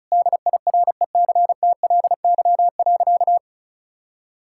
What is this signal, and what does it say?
Morse code, heard as a keyed tone, about 36 wpm, 700 Hz. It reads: DIRECTLY.
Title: Sentences from Top 1000 Words - Review Part 5 - 35wpm